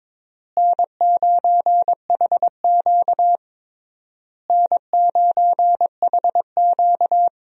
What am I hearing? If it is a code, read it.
Morse code, heard as a keyed tone, about 21 wpm, 700 Hz. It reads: N9HQ N9HQ
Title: USA Call Signs - 4 Characters - Repeated 3x - 22wpm